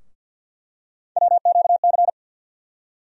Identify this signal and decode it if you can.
Morse code, heard as a keyed tone, about 52 wpm, 700 Hz. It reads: WQC
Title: USA Call Sign Postfixes - 3 Letters 50wpm